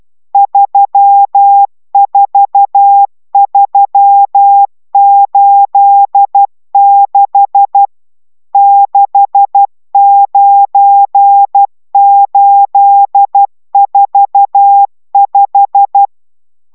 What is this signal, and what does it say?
Morse code, heard as a keyed tone, about 12 wpm, 800 Hz. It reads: 34386 69845